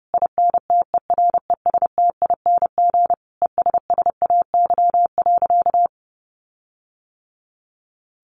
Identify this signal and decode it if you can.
Morse code, heard as a keyed tone, about 30 wpm, 700 Hz. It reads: INTERESTING ESSAY.
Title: Sentences from Top 1500 Words 30wpm